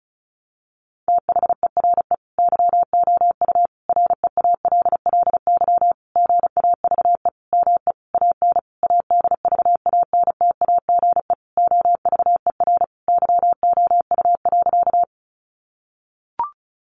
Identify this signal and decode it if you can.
Morse code, heard as a keyed tone, about 34 wpm, 700 Hz. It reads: THERE YOU REALLY GAVE ME AN ADVANTAGE OVER YOU.
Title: Sentences from Top 900 Words 35wpm